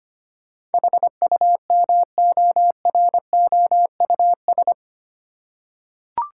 Morse code, heard as a keyed tone, about 25 wpm, 700 Hz. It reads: HUMOROUS
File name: 8 Letter Words - First 3 Letters Given - 25wpm